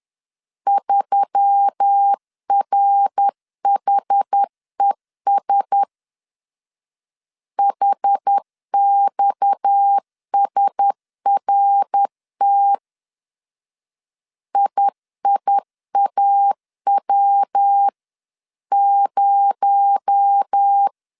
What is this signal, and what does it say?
Morse code, heard as a keyed tone, about 10 wpm, 800 Hz. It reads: 3RHES HXSRT IIAW0